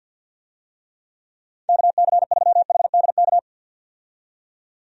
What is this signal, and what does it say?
Morse code, heard as a keyed tone, about 52 wpm, 700 Hz. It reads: KC3HDK
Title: USA Call Signs - 6 Characters - 50wpm